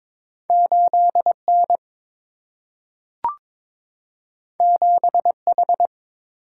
Morse code, heard as a keyed tone, about 22 wpm, 700 Hz. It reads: 8 N 7 H
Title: Single-Letter Single-Number 22wpm